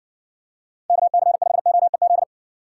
Morse code, heard as a keyed tone, about 58 wpm, 700 Hz. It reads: KC5OEZ